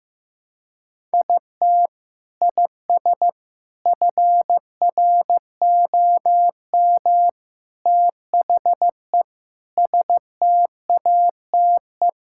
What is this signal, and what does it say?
Morse code, heard as a keyed tone, about 16 wpm, 700 Hz. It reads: IT IS FROM THE STATE
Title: Sentences from Top 1600 Words 15wpm